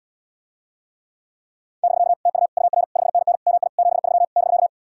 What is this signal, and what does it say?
Morse code, heard as a keyed tone, about 77 wpm, 700 Hz. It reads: <BK> TU UR 5NN MI 73 <BK>